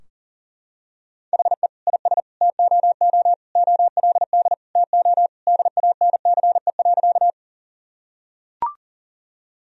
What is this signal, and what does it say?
Morse code, heard as a keyed tone, about 40 wpm, 700 Hz. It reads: HE IS TOO OLD TO DANCE.
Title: Sentences from Top 1000 Words - Review Part 4 - 40wpm